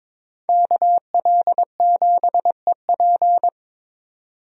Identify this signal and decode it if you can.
Morse code, heard as a keyed tone, about 22 wpm, 700 Hz. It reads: KL7EP
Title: DXCC-100 Call Signs - 5 Characters - 22wpm